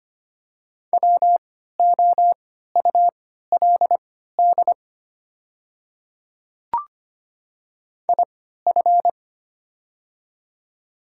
Morse code, heard as a keyed tone, about 25 wpm, spaced as for 14 wpm, 700 Hz. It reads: WOULD IF
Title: 3x Character Spacing - Top 100 Words - 25wpm